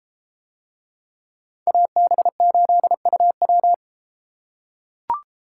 Morse code, heard as a keyed tone, about 33 wpm, 700 Hz. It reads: AB8UW